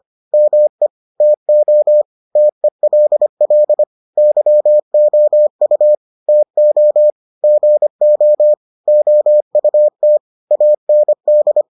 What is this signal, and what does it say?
Morse code, heard as a keyed tone, about 25 wpm, 600 Hz. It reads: ME TO TELL YOU TO GO OUT AND